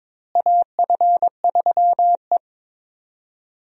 Morse code, heard as a keyed tone, about 22 wpm, 700 Hz. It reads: AF3E
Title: USA Call Signs - 4 Characters - Repeated 3x - 22wpm